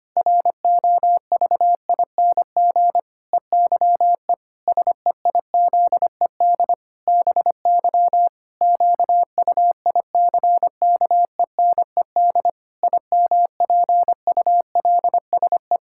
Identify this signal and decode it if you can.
Morse code, heard as a keyed tone, about 25 wpm, 700 Hz. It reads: ROVING EYE SEIZED BY QUICKENED IMPULSE
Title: Pangram Challenge v2 - 25wpm